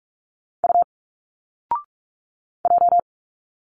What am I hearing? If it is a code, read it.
Morse code, heard as a keyed tone, about 46 wpm, 700 Hz. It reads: U J
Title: Single Letters 45wpm